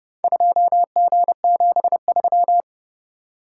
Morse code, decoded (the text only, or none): 2G73